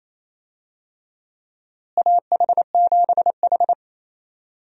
Morse code, heard as a keyed tone, about 28 wpm, 700 Hz. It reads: AH7H
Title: USA Call Signs - 4 Characters - Repeated 3x - 28wpm